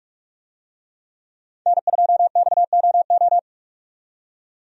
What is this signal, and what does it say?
Morse code, heard as a keyed tone, about 46 wpm, 700 Hz. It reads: N2XOO
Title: USA Call Signs - 5 Characters - 45wpm